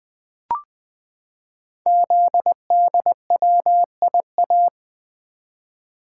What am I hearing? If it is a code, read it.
Morse code, heard as a keyed tone, about 20 wpm, 700 Hz. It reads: ZDWIA
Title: Code Groups - 5 Letters - 20wpm